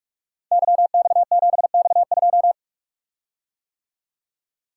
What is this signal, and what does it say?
Morse code, heard as a keyed tone, about 44 wpm, 700 Hz. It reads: YX7X2
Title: Code Groups - 5 Letter:Numbers - 45wpm